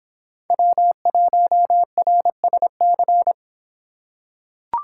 Morse code, heard as a keyed tone, about 26 wpm, 700 Hz. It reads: W1RSC